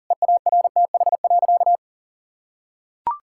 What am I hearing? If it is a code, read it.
Morse code, heard as a keyed tone, about 40 wpm, 700 Hz. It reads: EARTH.
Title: Sentences from Top 1000 Words - Review Part 5 - 40wpm